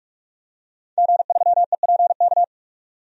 Morse code, heard as a keyed tone, about 46 wpm, 700 Hz. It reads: G3EPK
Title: DXCC-100 Call Signs - 5 Characters - 45wpm